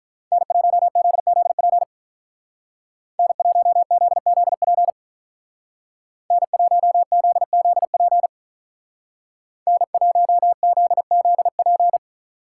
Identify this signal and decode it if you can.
Morse code, heard as a keyed tone, about 44 wpm, 700 Hz. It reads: N1ZZP N1ZZP N1ZZP N1ZZP